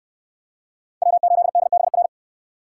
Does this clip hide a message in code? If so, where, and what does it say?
Yes: Morse code from 1.0 to 2.1 s, heard as a keyed tone, about 63 wpm, 700 Hz. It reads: W8DBR